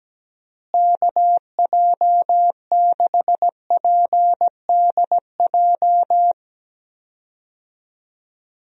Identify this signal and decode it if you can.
Morse code, heard as a keyed tone, about 17 wpm, 700 Hz. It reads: KJ6PDJ
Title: USA Call Signs - 6 Characters - Repeated 2x - 17wpm